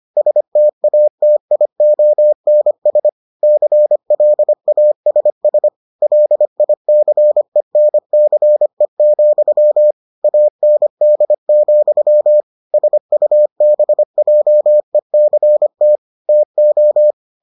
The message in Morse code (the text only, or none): STATIONS CLASS LICENCE, AND, SUBJECT TO